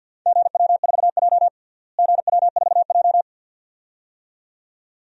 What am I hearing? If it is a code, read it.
Morse code, heard as a keyed tone, about 46 wpm, 700 Hz. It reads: GW4J GW4J
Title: DXCC-100 Call Signs - 4 Characters - Repeated 2x - 50wpm